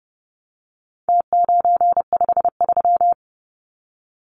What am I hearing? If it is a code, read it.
Morse code, heard as a keyed tone, about 30 wpm, 700 Hz. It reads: T953
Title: Numbers - Contest Style - 4-Digit 30wpm